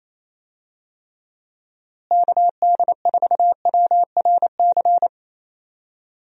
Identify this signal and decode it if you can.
Morse code, heard as a keyed tone, about 28 wpm, 700 Hz. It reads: KD4WRC
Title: USA Call Signs - 6 Characters - Repeated 3x - 28wpm